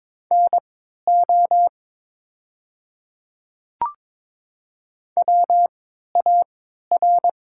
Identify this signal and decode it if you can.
Morse code, heard as a keyed tone, about 22 wpm, spaced as for 13 wpm, 700 Hz. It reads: NO WAR
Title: 3x Character Spacing - Top 500 Words - 22wpm